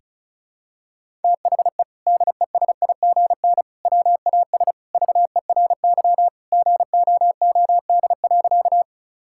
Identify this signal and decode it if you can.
Morse code, heard as a keyed tone, about 36 wpm, 700 Hz. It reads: THE DESIGN WAS VERY GOOD.
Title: Sentences from Top 1000 Words - Review Part 5 - 35wpm